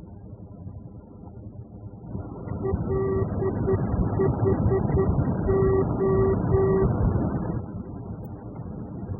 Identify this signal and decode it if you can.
Morse code, heard as a keyed tone, about 10 wpm, 380 Hz, with noise. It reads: LHO